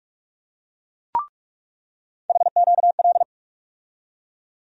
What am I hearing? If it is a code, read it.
Morse code, heard as a keyed tone, about 42 wpm, 700 Hz. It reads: HQL
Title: USA Call Sign Postfixes - 3 Letters 45wpm